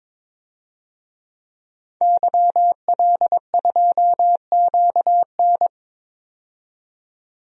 Morse code, heard as a keyed tone, about 23 wpm, 700 Hz. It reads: YL2QN
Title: DXCC-100 Call Signs - 5 Characters - 22wpm